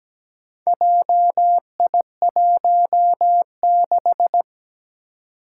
Morse code, heard as a keyed tone, about 17 wpm, 700 Hz. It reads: JI16
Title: Four Challenging Letter-Number Combinations 17wpm